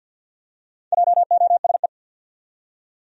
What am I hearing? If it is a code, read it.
Morse code, heard as a keyed tone, about 50 wpm, 700 Hz. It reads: JOSE
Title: USA Common Names - 50wpm